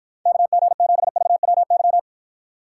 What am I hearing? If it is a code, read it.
Morse code, heard as a keyed tone, about 52 wpm, 700 Hz. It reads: KG7VWY